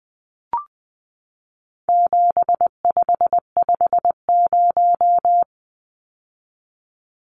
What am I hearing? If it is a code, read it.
Morse code, heard as a keyed tone, about 20 wpm, 700 Hz. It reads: 7550
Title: Numbers - 4-Digit 20wpm